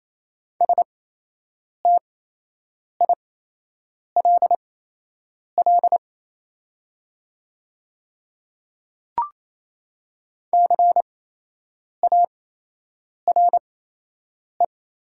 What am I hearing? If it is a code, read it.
Morse code, heard as a keyed tone, about 28 wpm, spaced as for 8 wpm, 700 Hz. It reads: STILL CARE
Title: 8x Character Spacing - Top 300 Words - 28wpm